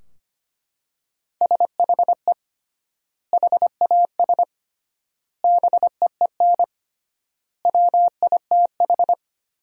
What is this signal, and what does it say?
Morse code, heard as a keyed tone, about 25 wpm, 700 Hz. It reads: SHE HAS BEEN WITH
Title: 3x Word-Spacing - Sentences from Top 200 Words - 25wpm